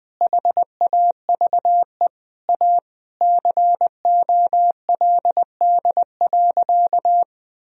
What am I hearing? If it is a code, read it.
Morse code, heard as a keyed tone, about 20 wpm, 700 Hz. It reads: HAVE A COLD.
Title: Sentences from Top 1000 Words - Review Part 3 - 20wpm